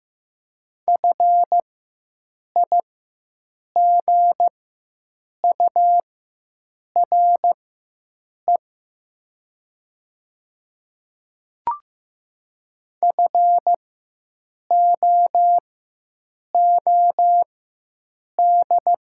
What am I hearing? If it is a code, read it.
Morse code, heard as a keyed tone, about 15 wpm, spaced as for 7 wpm, 700 Hz. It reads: FIGURE FOOD
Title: 4x Character Spacing - Top 500 Words - 15wpm